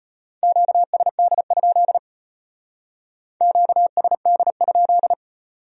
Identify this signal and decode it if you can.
Morse code, heard as a keyed tone, about 32 wpm, 700 Hz. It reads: QSD? QSD?